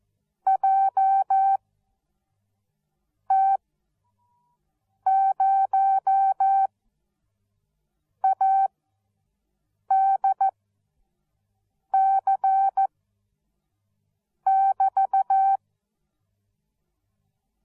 Morse code, heard as a keyed tone, about 14 wpm, spaced as for 5 wpm, 780 Hz. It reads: JT0ADC<BT>